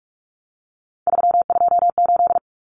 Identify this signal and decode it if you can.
Morse code, heard as a keyed tone, about 44 wpm, 700 Hz. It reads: 328
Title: Common Cut Numbers - 3-Digit 45wpm